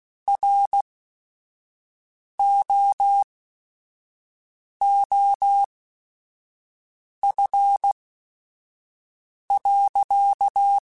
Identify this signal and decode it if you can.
Morse code, heard as a keyed tone, about 16 wpm, spaced as for 5 wpm, 800 Hz. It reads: ROOF.